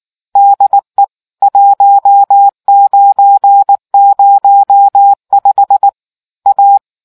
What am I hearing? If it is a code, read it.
Morse code, heard as a keyed tone, about 19 wpm, 800 Hz. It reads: DE 1905 A